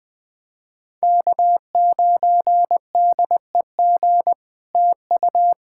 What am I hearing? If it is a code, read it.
Morse code, heard as a keyed tone, about 20 wpm, 700 Hz. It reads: K9DEG TU
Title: POTA QSO Practice - 20wpm